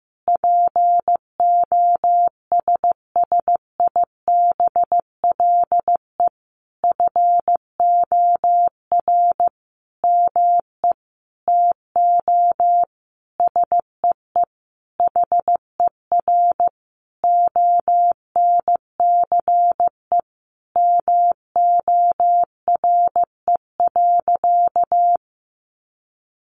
Morse code, heard as a keyed tone, about 15 wpm, 700 Hz. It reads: POSSIBLE FOR ME TO SEE HER ONCE MORE.
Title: Sentences from Top 500 Words 15wpm